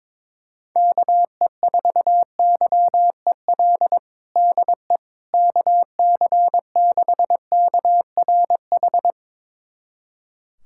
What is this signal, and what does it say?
Morse code, heard as a keyed tone, about 22 wpm, 700 Hz. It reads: KE4YEL DE KC6KRH
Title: USA Callsign DE Callsign 22wpm